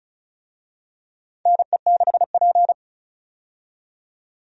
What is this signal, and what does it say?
Morse code, heard as a keyed tone, about 34 wpm, 700 Hz. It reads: NE6P
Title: USA Call Signs - 4 Characters - 35wpm